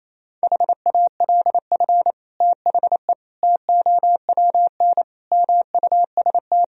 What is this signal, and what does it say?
Morse code, heard as a keyed tone, about 28 wpm, 700 Hz. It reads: HALF THE TOWN MUST
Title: Sentences from Top 1000 Words - Review Part 2 - 28wpm